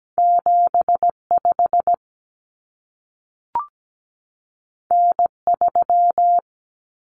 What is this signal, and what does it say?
Morse code, heard as a keyed tone, about 17 wpm, 700 Hz. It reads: 75 N3
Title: Common Cut Numbers - 2-Digit 17wpm